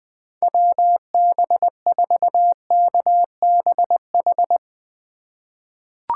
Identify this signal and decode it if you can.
Morse code, heard as a keyed tone, about 21 wpm, 700 Hz. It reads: WB4KBH
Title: USA Call Signs - 3x Character Spacing - 20wpm